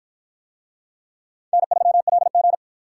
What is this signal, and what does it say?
Morse code, heard as a keyed tone, about 53 wpm, 700 Hz. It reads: N3LG